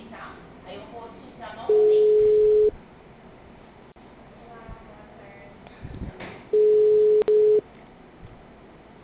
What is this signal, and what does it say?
Ringback tone (Europe, 425 Hz), with speech.